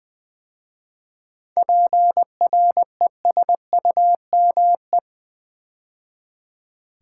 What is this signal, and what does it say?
Morse code, heard as a keyed tone, about 20 wpm, 700 Hz. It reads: PRESUME
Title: 7 Letter Words - First 3 Letters Given - 20wpm